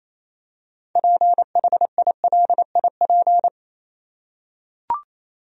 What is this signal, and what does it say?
Morse code, heard as a keyed tone, about 28 wpm, 700 Hz. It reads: PHILIP
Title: USA Common Names - 28wpm